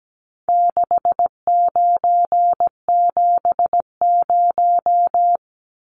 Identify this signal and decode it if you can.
Morse code, heard as a keyed tone, about 17 wpm, 700 Hz. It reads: 6970
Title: Numbers - 4-Digit 17wpm